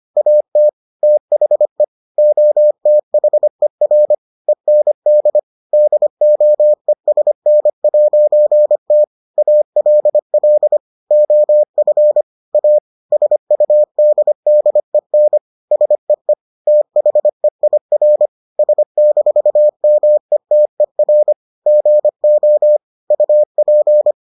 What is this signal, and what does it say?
Morse code, heard as a keyed tone, about 25 wpm, 600 Hz. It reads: AT THE OTHER END DOESN'T ALL OF A SUDDEN SEE THEIR S-METER GO UP